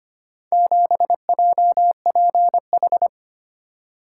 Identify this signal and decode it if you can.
Morse code, heard as a keyed tone, about 25 wpm, 700 Hz. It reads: 7JPH
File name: practice / 25wpm / Four Letter-Number Combinations - 25wpm